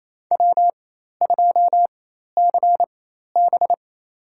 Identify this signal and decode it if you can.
Morse code, heard as a keyed tone, about 28 wpm, spaced as for 13 wpm, 700 Hz. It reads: W2CB
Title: USA Call Signs - 4 Characters - 4x Character Spacing - 28wpm